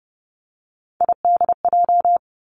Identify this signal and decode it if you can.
Morse code, heard as a keyed tone, about 30 wpm, 700 Hz. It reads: IDJ
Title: Three Challenging Letter-Number Combinations 30wpm